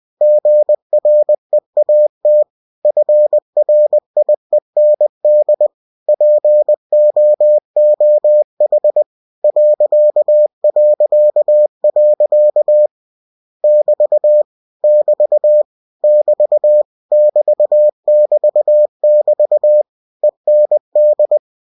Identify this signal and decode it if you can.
Morse code, heard as a keyed tone, about 20 wpm, 600 Hz. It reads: GREAT FRIEND POOH ... <BT> <BT> <BT> <BT> <BT> <BT> END
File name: cw-books / winnie-the-pooh / winnie-20wpm-page-0111